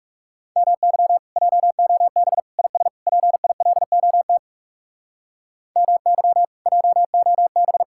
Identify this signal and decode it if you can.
Morse code, heard as a keyed tone, about 38 wpm, 700 Hz. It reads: MY JOB IS PILOT MY JOB